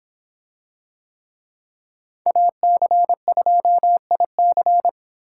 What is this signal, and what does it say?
Morse code, heard as a keyed tone, about 25 wpm, 700 Hz. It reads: AC2IC